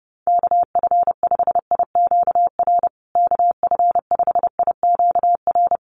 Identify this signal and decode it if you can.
Morse code, heard as a keyed tone, about 30 wpm, 700 Hz. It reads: KF5IQR KF5IQR
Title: USA Call Signs Repeated 3x 30wpm